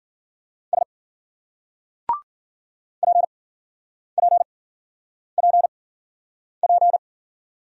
Morse code, heard as a keyed tone, about 47 wpm, spaced as for 9 wpm, 700 Hz. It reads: S PPPP
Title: Speed-Racing Single Letters 40wpm